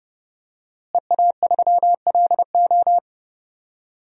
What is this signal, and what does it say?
Morse code, heard as a keyed tone, about 30 wpm, 700 Hz. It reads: EA3LO
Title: DXCC-100 Call Signs - 5 Characters - Repeated 3x - 30wpm